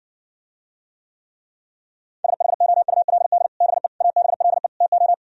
Morse code, heard as a keyed tone, about 59 wpm, 700 Hz. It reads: SHOULD BE ABLE TO